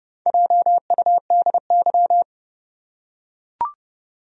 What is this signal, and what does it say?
Morse code, heard as a keyed tone, about 30 wpm, 700 Hz. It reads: JUDY
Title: USA Common Names - Max 5 Letters - 30wpm